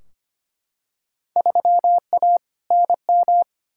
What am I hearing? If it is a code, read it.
Morse code, heard as a keyed tone, about 25 wpm, 700 Hz. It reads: 3A NM
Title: Field Day Exchanges Encore - 25wpm